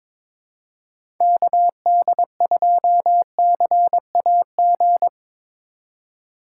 Morse code, heard as a keyed tone, about 22 wpm, 700 Hz. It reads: KD2CAG
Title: USA Call Signs - 6 Characters - 4x Character Spacing - 22wpm